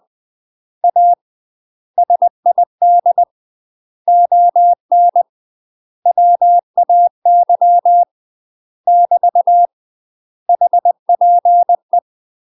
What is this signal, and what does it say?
Morse code, heard as a keyed tone, about 20 wpm, 700 Hz. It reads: A SID ON WAY <BT> HPE
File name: UK MASTER QSOs TRAINING / 20WPM / UK MASTER QSOs 0001